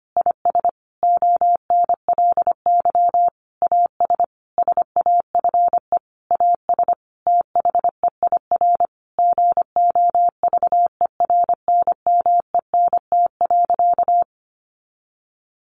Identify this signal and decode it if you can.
Morse code, heard as a keyed tone, about 25 wpm, 700 Hz. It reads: IS ONLY AS SAFE AS THEIR GOVERNMENT.
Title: Sentences from Top 1500 Words 25wpm